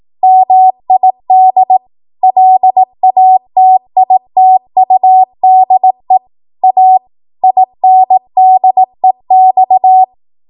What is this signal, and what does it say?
Morse code, heard as a keyed tone, about 18 wpm, 750 Hz. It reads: MID LATITUDE A INDEX
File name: Archive / 18 WPM / 18WPM